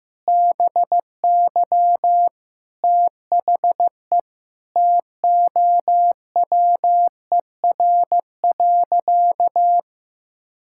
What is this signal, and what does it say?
Morse code, heard as a keyed tone, about 15 wpm, 700 Hz. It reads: BY THE TOWER.
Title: Pangram Challenge v2 - 15wpm